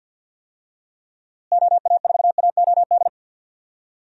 Morse code, heard as a keyed tone, about 50 wpm, 700 Hz. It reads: OA4AOD
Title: USA Call Signs - 6 Characters - Repeated 2x - 50wpm